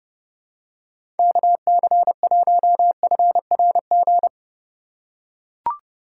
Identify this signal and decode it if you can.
Morse code, heard as a keyed tone, about 32 wpm, 700 Hz. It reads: KC1FRG
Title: USA Call Signs - 6 Characters - 3x Character Spacing - 30wpm